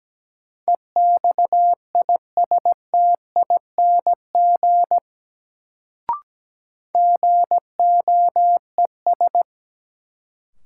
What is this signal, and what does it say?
Morse code, heard as a keyed tone, about 17 wpm, 700 Hz. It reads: EXISTING GOES
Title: Sentences from Top 1500 Words - New Word Drill Alt - 17wpm